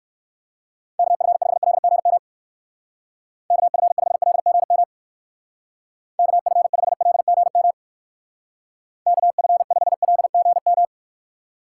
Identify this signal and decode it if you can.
Morse code, heard as a keyed tone, about 59 wpm, 700 Hz. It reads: KF5LGM KF5LGM KF5LGM KF5LGM